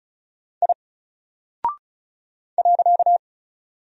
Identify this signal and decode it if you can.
Morse code, heard as a keyed tone, about 36 wpm, 700 Hz. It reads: I .
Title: Single Character (Letter:Number:Punctuation:Prosign) Alt 35wpm